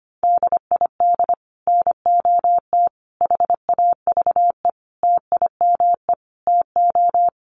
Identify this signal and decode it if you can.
Morse code, heard as a keyed tone, about 25 wpm, 700 Hz. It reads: DID NOT HAVE TIME TO
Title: Sets of 5 Words Encore 25wpm